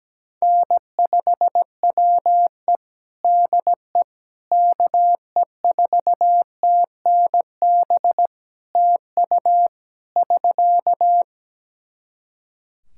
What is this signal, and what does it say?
Morse code, heard as a keyed tone, about 17 wpm, 700 Hz. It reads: N5WE DE KE4TNB TU <SK>
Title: K1USN SST QSO Practice 17wpm